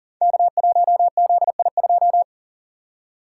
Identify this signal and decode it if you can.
Morse code, heard as a keyed tone, about 42 wpm, 700 Hz. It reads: K1ZI2
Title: Code Groups - 5 Letter:Numbers - 40wpm